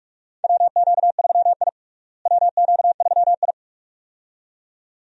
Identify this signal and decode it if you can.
Morse code, heard as a keyed tone, about 46 wpm, 700 Hz. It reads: WQ3I WQ3I
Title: USA Call Signs - 4 Characters - Repeated 2x - 45wpm